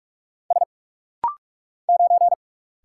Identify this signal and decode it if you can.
Morse code, heard as a keyed tone, about 45 wpm, 700 Hz. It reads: S 9
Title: Single Letter-Number Alt - Rapid-Fire 45wpm